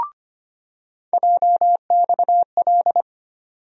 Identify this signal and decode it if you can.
Morse code, heard as a keyed tone, about 25 wpm, 700 Hz. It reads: JXL
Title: USA Call Sign Postfixes - 3 Letters 25wpm